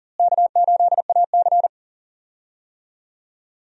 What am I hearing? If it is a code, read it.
Morse code, heard as a keyed tone, about 40 wpm, 700 Hz. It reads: K8AC
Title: USA Call Signs - 4 Characters - Repeated 2x - 40wpm